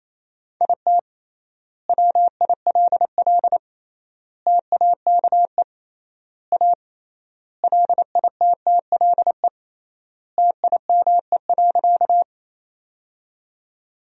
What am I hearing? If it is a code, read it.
Morse code, heard as a keyed tone, about 28 wpm, 700 Hz. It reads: IT WILL TAKE A LITTLE TIME.
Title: 3x Word-Spacing - Sentences from Top 200 Words - 28wpm